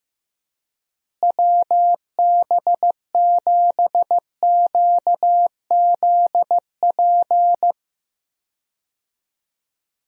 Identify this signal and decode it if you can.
Morse code, heard as a keyed tone, about 15 wpm, 700 Hz. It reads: WB7QZP
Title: USA Call Signs - 6 Characters - Repeated 2x - 15wpm